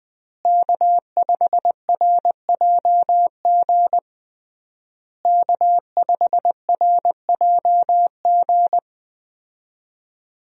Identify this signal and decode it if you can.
Morse code, heard as a keyed tone, about 21 wpm, 700 Hz. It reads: K5RJG K5RJG
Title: USA Call Signs - 5 Characters - Repeated 3x - 20wpm